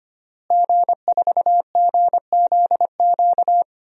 Morse code, heard as a keyed tone, about 25 wpm, 700 Hz. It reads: G4GZQ
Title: DXCC-100 Call Signs - 5 Characters - Repeated 3x - 25wpm